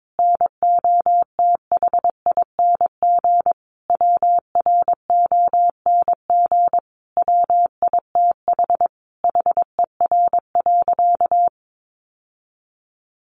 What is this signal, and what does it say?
Morse code, heard as a keyed tone, about 22 wpm, 700 Hz. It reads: NOTHING WRONG WITH HER.
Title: Sentences from Top 900 Words 22wpm